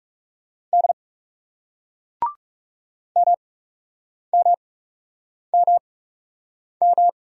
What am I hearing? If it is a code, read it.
Morse code, heard as a keyed tone, about 39 wpm, spaced as for 8 wpm, 700 Hz. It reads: D MMMM